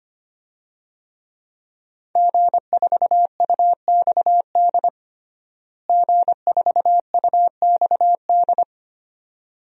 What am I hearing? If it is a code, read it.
Morse code, heard as a keyed tone, about 25 wpm, 700 Hz. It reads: G4UXD G4UXD
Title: DXCC-100 Call Signs - 5 Characters - Repeated 2x - 25wpm